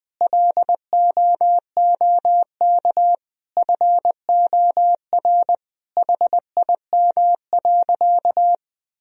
Morse code, heard as a keyed tone, about 20 wpm, 700 Hz. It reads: LOOK FOR HIM.
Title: Sentences from Top 1000 Words - Review Part 4 - 20wpm